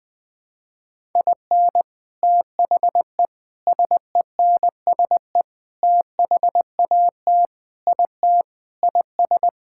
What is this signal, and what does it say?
Morse code, heard as a keyed tone, about 20 wpm, 700 Hz. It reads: IN THE SENSE THAT IT IS
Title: Sentences with 4-Word Phrases - Spoken Beginning - 20wpm